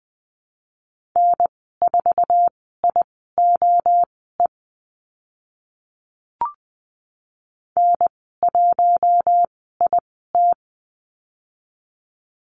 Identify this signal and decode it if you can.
Morse code, heard as a keyed tone, about 20 wpm, spaced as for 15 wpm, 700 Hz. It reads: N4IOE N1IT
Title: 062 Lesson 11g - Letter h - Call Signs 20wpm